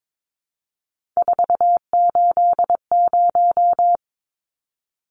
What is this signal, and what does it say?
Morse code, heard as a keyed tone, about 22 wpm, 700 Hz. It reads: 480
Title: Numbers - 3-Digit 22wpm